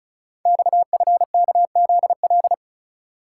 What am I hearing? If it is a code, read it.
Morse code, heard as a keyed tone, about 36 wpm, 700 Hz. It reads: XFKZL
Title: Code Groups - 5 Letters - 35wpm